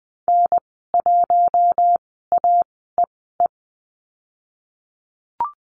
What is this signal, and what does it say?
Morse code, heard as a keyed tone, about 20 wpm, spaced as for 14 wpm, 700 Hz. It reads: N1AEE
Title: 055 Lesson 10g - Letter r - Call Signs 20wpm